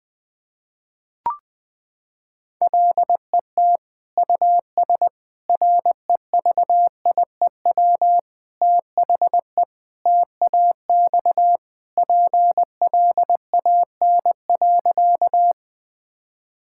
Morse code, heard as a keyed tone, about 20 wpm, 700 Hz. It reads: LET US REVIEW THE TAX PLAN.
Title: Sentences from Top 1000 Words - Review Part 6 - 20wpm